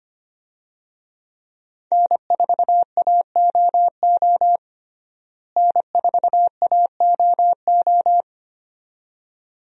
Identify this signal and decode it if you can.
Morse code, heard as a keyed tone, about 25 wpm, 700 Hz. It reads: N4AOO N4AOO
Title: USA Call Signs - 5 Characters - Repeated 2x - 25wpm